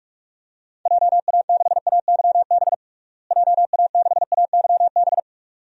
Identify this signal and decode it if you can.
Morse code, heard as a keyed tone, about 42 wpm, 700 Hz. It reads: JA6AYB JA6AYB
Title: DXCC-100 Call Signs - 6 Characters - Repeated 2x - 45wpm